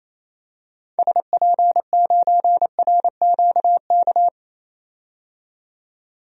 Morse code, heard as a keyed tone, about 28 wpm, 700 Hz. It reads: SP9RQK
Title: DXCC-100 Call Signs - 6 Characters - 28wpm